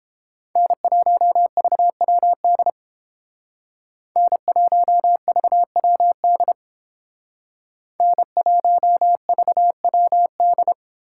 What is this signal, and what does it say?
Morse code, heard as a keyed tone, about 30 wpm, 700 Hz. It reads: N1VWD N1VWD N1VWD